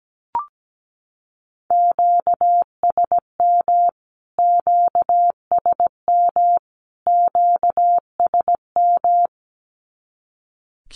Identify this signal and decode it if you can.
Morse code, heard as a keyed tone, about 17 wpm, 700 Hz. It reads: QSM QSM QSM
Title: Q-Codes Repeated 3x 17wpm